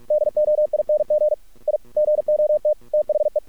Morse code, heard as a keyed tone, about 46 wpm, 600 Hz. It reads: DOING I GOT THE
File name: archived / CRSnet